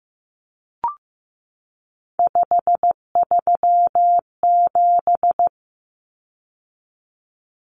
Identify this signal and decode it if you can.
Morse code, heard as a keyed tone, about 15 wpm, 700 Hz. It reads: 537
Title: Numbers - 3-Digit 15wpm